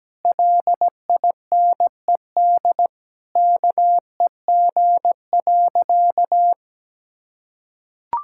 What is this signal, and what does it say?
Morse code, heard as a keyed tone, about 17 wpm, 700 Hz. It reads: LINED KEG.
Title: Pangram Challenge v2 - 17wpm